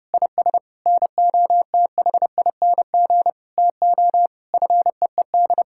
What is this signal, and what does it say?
Morse code, heard as a keyed tone, about 32 wpm, 700 Hz. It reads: IS NOTHING TO FEED
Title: Sentences from Top 1600 Words 30wpm